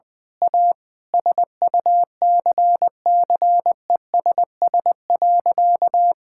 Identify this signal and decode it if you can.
Morse code, heard as a keyed tone, about 20 wpm, 700 Hz. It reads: A SUCCESS.
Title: Sentences with 9-Word Phrases - 20wpm